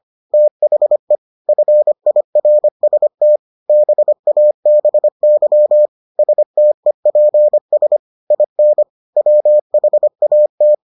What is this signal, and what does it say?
Morse code, heard as a keyed tone, about 25 wpm, 600 Hz. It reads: THE FIRST BABY STEPS IN WHAT